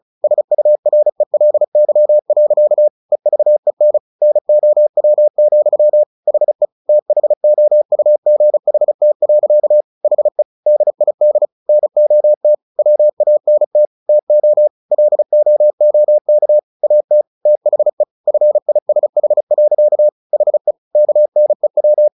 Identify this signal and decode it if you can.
Morse code, heard as a keyed tone, about 36 wpm, 600 Hz. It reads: SURELY. EVEN NOW, HE THOUGHT. HE DID NOT WANT TO LOOK AT THE FISH. HE KNEW
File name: the old man and the sea 2 35wpm